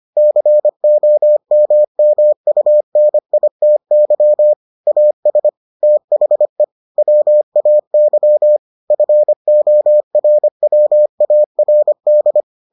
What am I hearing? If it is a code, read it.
Morse code, heard as a keyed tone, about 25 wpm, 600 Hz. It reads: COMMUNITY AS THE WAY FORWARD